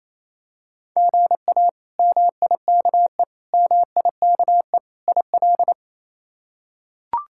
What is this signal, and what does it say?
Morse code, heard as a keyed tone, about 28 wpm, 700 Hz. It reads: GA MIKE MIKE IL
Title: K1USN SST QSO Practice - 28wpm